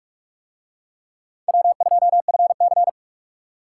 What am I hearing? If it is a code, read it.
Morse code, heard as a keyed tone, about 39 wpm, 700 Hz. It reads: W2FC